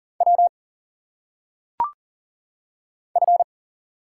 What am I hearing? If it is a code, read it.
Morse code, heard as a keyed tone, about 40 wpm, 700 Hz. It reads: W F
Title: Single Letters - Phonetic Pronunciation 40wpm